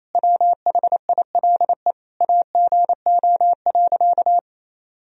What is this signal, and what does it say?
Morse code, heard as a keyed tone, about 28 wpm, 700 Hz. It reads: WHILE AGO.
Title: Sentences from Top 1000 Words - Review Part 1 - 28wpm